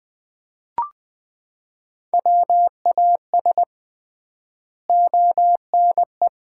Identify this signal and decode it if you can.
Morse code, heard as a keyed tone, about 20 wpm, 700 Hz. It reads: WAS ONE
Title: 3x Word-Spacing - Sets of 5 Words - 20wpm